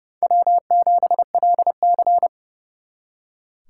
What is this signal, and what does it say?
Morse code, heard as a keyed tone, about 36 wpm, 700 Hz. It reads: W7LC
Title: Speed Racing - USA 1x2 Call Signs 30wpm